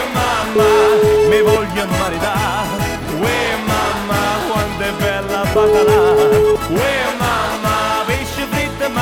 Ringback tone (Europe, 425 Hz), with noise.